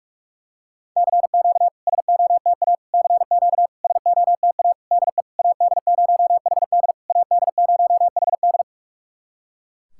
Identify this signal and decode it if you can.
Morse code, heard as a keyed tone, about 46 wpm, 690 Hz. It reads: CQ SOTA CQ SOTA DE AD0HD AD0HD
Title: SOTA QSO Practice 45wpm